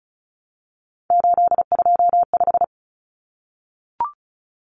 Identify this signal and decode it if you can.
Morse code, heard as a keyed tone, about 34 wpm, 700 Hz. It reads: 825
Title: Common Cut Numbers - 3-Digit 35wpm